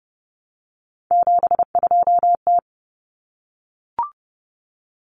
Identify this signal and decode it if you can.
Morse code, heard as a keyed tone, about 30 wpm, 700 Hz. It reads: 72T
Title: Common Cut Numbers - 3-Digit 30wpm